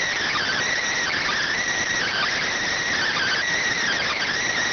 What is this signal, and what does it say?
Emergency Alert System (SAME) tones, heard as harsh buzzing bursts. It is heard from 0.4 to 4.3 s.